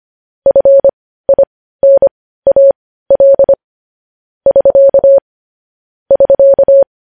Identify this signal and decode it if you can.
Morse code, heard as a keyed tone, about 25 wpm, spaced as for 15 wpm, 550 Hz. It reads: FINAL <SK> <SK>
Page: News headlines at 15